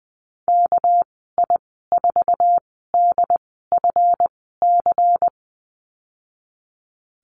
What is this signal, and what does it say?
Morse code, heard as a keyed tone, about 20 wpm, spaced as for 15 wpm, 700 Hz. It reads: KI4DFC
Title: 231 Lesson 37g - Letter x - Call Signs 20wpm